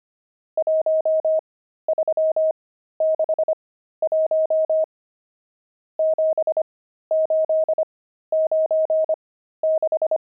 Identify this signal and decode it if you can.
Morse code, heard as a keyed tone, about 25 wpm, spaced as for 13 wpm, 630 Hz. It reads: 1361 7896
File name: pangram-fs-13-cs-25-nums4